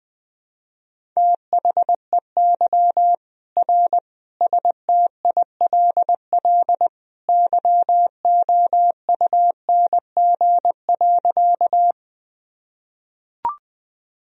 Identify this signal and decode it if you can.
Morse code, heard as a keyed tone, about 21 wpm, 700 Hz. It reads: THEY R STILL YOUNG.
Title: Sentences from Top 400 Words and Abbreviations 20wpm